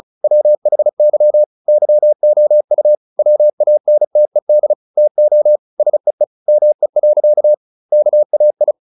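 Morse code, heard as a keyed tone, about 34 wpm, 600 Hz. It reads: WHY YOU WANTED TO SEE ME. KAI
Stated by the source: Morse practice at 35 wpm from CW med Gnister, sending Ken Follett's novel Never in English.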